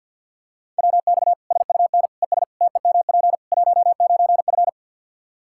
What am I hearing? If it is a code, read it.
Morse code, heard as a keyed tone, about 52 wpm, 700 Hz. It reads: WX SUN ES TEMP 19F
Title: Practice QSO - Exchange 2 - Info - 50wpm